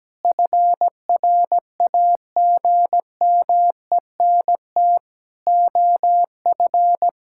Morse code, heard as a keyed tone, about 17 wpm, 700 Hz. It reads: FRAGMENT OF
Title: Pangram Challenge v2 - 17wpm